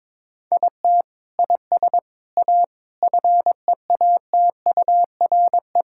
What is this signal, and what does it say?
Morse code, heard as a keyed tone, about 21 wpm, 700 Hz. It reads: IT IS A FEATURE
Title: Sentences from Top 1000 Words - Review Part 5 - 22wpm